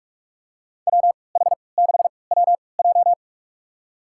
Morse code, heard as a keyed tone, about 46 wpm, spaced as for 25 wpm, 700 Hz. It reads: WH6WJ
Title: USA Call Signs - 3x Character Spacing - 45wpm